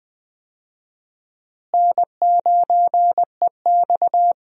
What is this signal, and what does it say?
Morse code, heard as a keyed tone, about 21 wpm, 700 Hz. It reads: N9EX
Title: USA Call Signs - 4 Characters - Repeated 3x - 20wpm